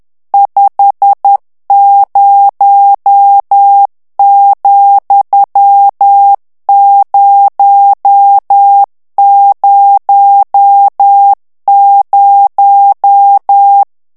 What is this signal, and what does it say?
Morse code, heard as a keyed tone, about 10 wpm, 800 Hz. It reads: 50,000
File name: morse mp3 / ANGLAIS texte clair 2 vitesse 10 mots minute